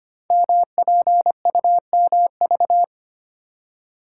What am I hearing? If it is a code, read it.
Morse code, heard as a keyed tone, about 25 wpm, 700 Hz. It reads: MPUMV
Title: Code Groups - 5 Letters - 25wpm